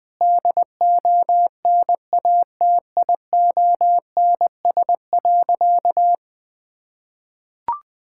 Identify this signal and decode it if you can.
Morse code, heard as a keyed tone, about 20 wpm, 700 Hz. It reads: DONATIONS.
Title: Sentences with 9-Word Phrases - Spoken Beginning - 20wpm